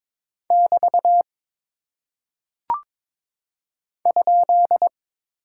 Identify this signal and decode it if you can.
Morse code, heard as a keyed tone, about 22 wpm, 700 Hz. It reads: <BT> ?
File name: practice / 22wpm / Single Punctuation-Prosign - 22wpm